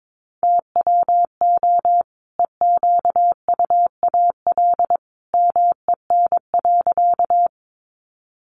Morse code, heard as a keyed tone, about 22 wpm, 700 Hz. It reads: TWO EQUAL MEN.
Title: Sentences from Top 1400 Words 22wpm